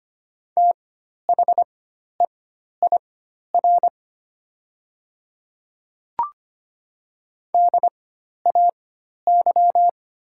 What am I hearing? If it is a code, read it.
Morse code, heard as a keyed tone, about 25 wpm, spaced as for 12 wpm, 700 Hz. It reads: THEIR DAY
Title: 4x Character Spacing - Top 100 Words - 25wpm